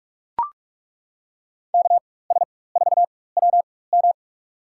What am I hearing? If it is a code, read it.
Morse code, heard as a keyed tone, about 41 wpm, spaced as for 20 wpm, 700 Hz. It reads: KS4WM
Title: USA Call Signs - 5 Characters - 4x Character Spacing - 45wpm